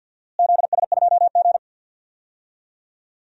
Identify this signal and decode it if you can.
Morse code, heard as a keyed tone, about 50 wpm, 700 Hz. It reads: ZS2G
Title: DXCC-100 Call Signs - 4 Characters - Repeated 3x - 50wpm